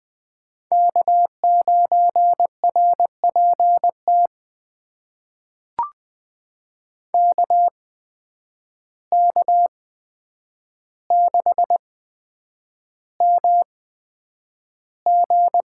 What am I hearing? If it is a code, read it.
Morse code, heard as a keyed tone, about 20 wpm, spaced as for 9 wpm, 700 Hz. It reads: K9RPT KK6MG
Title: USA Call Signs - 5 Characters - 8x Character Spacing - 20wpm